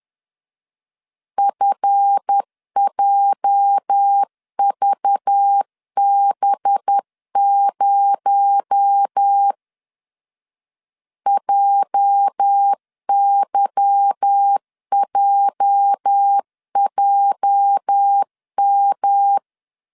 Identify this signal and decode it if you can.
Morse code, heard as a keyed tone, about 10 wpm, 800 Hz. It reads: FJVB0 JYJJM
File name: LEZIONI / 8 YJ45 / 8 YJ 10wpm 2651